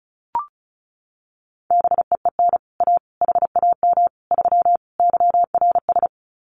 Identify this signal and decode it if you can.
Morse code, heard as a keyed tone, about 36 wpm, 700 Hz. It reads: BEEN A HAM 3 YRS
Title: Common QSO Phrases Encore 35wpm